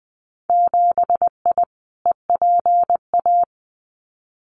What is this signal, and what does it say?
Morse code, heard as a keyed tone, about 20 wpm, 700 Hz. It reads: 7I EPA
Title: Winter Field Day Exchanges Encore 20wpm